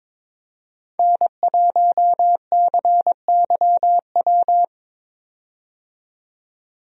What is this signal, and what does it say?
Morse code, heard as a keyed tone, about 22 wpm, 700 Hz. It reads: N1CYW
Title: USA Call Signs - 5 Characters - 22wpm